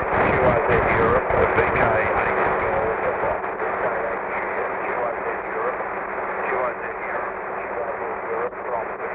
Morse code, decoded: none